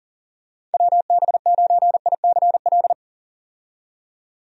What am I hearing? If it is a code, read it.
Morse code, heard as a keyed tone, about 40 wpm, 700 Hz. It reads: WB9ICL
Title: USA Call Signs - 6 Characters - 40wpm